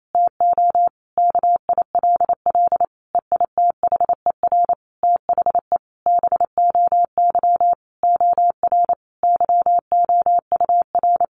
Morse code, heard as a keyed tone, about 28 wpm, 700 Hz. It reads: TO KILL EITHER THE BOY OR YOUR
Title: Sentences from Top 1500 Words 28wpm